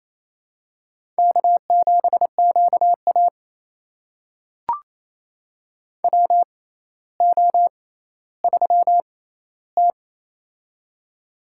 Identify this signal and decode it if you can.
Morse code, heard as a keyed tone, about 28 wpm, spaced as for 14 wpm, 700 Hz. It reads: K7QA WO3T
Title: USA Call Signs - 4 Characters - 6x Character Spacing - 28wpm